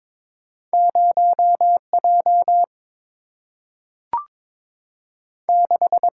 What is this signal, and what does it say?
Morse code, heard as a keyed tone, about 22 wpm, 700 Hz. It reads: 0 J 6
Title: Single-Letter Single-Number 22wpm